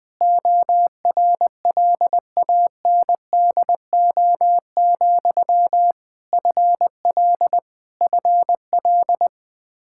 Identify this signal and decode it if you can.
Morse code, heard as a keyed tone, about 20 wpm, 700 Hz. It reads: ORLANDO, FL FL
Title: Practice QSO - Exchange 1 - Beginning - 20wpm